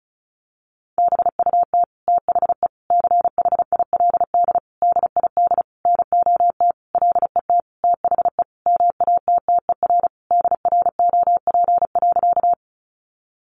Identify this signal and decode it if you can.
Morse code, heard as a keyed tone, about 34 wpm, 700 Hz. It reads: BUT THE CHILD DID NOT LET THE MATTER DROP.
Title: Sentences from Top 1400 Words 35wpm